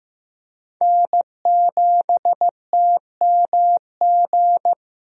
Morse code, heard as a keyed tone, about 15 wpm, 700 Hz. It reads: N7TMG
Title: USA Call Signs - 5 Characters - Repeated 3x - 15wpm